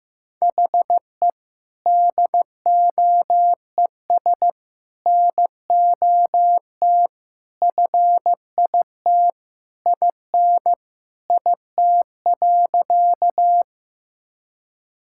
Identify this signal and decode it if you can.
Morse code, heard as a keyed tone, about 15 wpm, 700 Hz. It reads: HE DOES NOT FIT IN IT.
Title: Sentences from Top 1000 Words - Review Part 5 - 15wpm